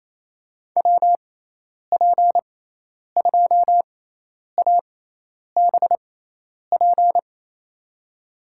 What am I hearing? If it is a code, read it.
Morse code, heard as a keyed tone, about 28 wpm, spaced as for 10 wpm, 700 Hz. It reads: WP2ABP
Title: USA Call Signs - 6 Characters - 6x Character Spacing - 28wpm